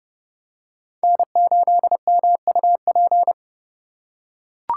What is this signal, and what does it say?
Morse code, heard as a keyed tone, about 30 wpm, 700 Hz. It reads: N8MUP
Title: USA Call Signs - 5 Characters - 3x Character Spacing - 30wpm